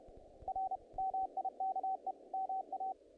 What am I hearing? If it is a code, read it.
Morse code, heard as a keyed tone, about 30 wpm, 760 Hz, with radio noise. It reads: R MIKE MA
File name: contests / naqp-cw-k5tr-op-k5pi- -1800